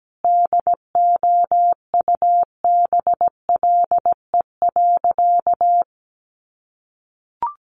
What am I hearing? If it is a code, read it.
Morse code, heard as a keyed tone, about 17 wpm, 700 Hz. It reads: DOUBLE.
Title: Sentences from Top 1400 Words 17wpm